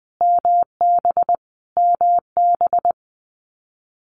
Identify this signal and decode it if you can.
Morse code, heard as a keyed tone, about 20 wpm, 700 Hz. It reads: MB MB
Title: Canadian Provinces and Territories - Repeated 3x - 20wpm